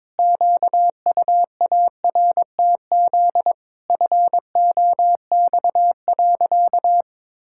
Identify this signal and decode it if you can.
Morse code, heard as a keyed tone, about 22 wpm, 700 Hz. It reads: QUARTZ FOX.
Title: Pangram Challenge v2 - 22wpm